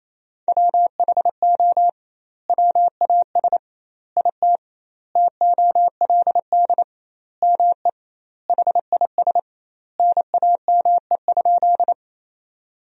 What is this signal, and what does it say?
Morse code, heard as a keyed tone, about 28 wpm, 700 Hz. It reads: WHO WAS IT TOLD ME HIS NAME?
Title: 2x Word-Spacing - Sentences from Top 400 Words - 28wpm